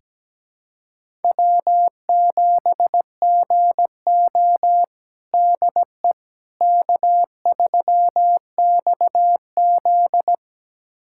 Morse code, heard as a keyed tone, about 17 wpm, 700 Hz. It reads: W7GO DE K3XZ
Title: USA Callsign DE Callsign - 4 Characters - 17wpm